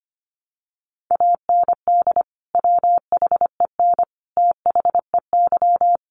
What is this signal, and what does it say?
Morse code, heard as a keyed tone, about 25 wpm, 700 Hz. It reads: AND WHEN THEY
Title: Sentences from Top 1100 Words 25wpm